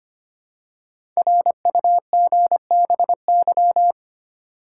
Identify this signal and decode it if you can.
Morse code, heard as a keyed tone, about 25 wpm, 700 Hz. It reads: RUGBY
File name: 5 Letter Words - First 3 Letters Given - 25wpm